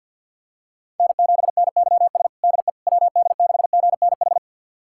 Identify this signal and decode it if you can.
Morse code, heard as a keyed tone, about 51 wpm, 700 Hz. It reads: N7NYS DE WD6GNH
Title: USA Callsign DE Callsign 50wpm